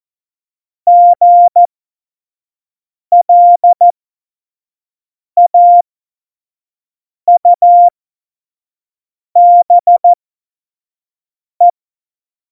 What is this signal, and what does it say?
Morse code, heard as a keyed tone, about 14 wpm, spaced as for 5 wpm, 700 Hz. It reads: GLAUBE
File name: Morsekurs / Witze / Witz 26. Dezember 2017, 25 F70